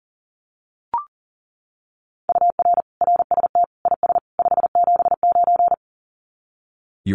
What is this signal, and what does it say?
Morse code, heard as a keyed tone, about 37 wpm, 700 Hz. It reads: UR RST IS 579
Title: Common QSO Phrases 40wpm